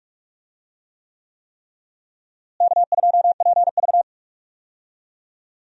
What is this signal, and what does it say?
Morse code, heard as a keyed tone, about 42 wpm, 700 Hz. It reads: K2PV
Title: USA Call Signs - 4 Characters - Repeated 3x - 45wpm